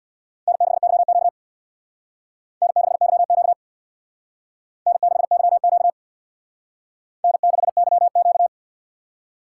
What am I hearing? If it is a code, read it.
Morse code, heard as a keyed tone, about 61 wpm, 700 Hz. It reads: N6YQ N6YQ N6YQ N6YQ
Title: Speed Racing - USA 1x2 Call Signs 50wpm